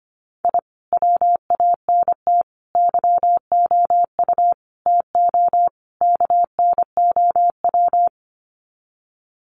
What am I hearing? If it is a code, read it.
Morse code, heard as a keyed tone, about 25 wpm, 700 Hz. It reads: I WANT YOU TO KNOW
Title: Sets of 5 Words Encore 25wpm